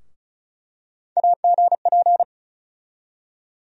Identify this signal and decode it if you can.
Morse code, heard as a keyed tone, about 35 wpm, 700 Hz. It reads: AGP
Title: USA Call Sign Postfixes - 3 Letters 35wpm